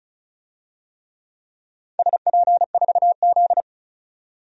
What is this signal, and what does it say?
Morse code, heard as a keyed tone, about 36 wpm, 700 Hz. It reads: SP4Z
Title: DXCC-100 Call Signs - 4 Characters - 35wpm